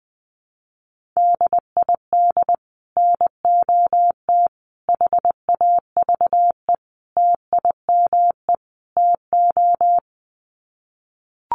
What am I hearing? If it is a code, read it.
Morse code, heard as a keyed tone, about 19 wpm, 700 Hz. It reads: DID NOT HAVE TIME TO
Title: Sets of 5 Words Encore 20wpm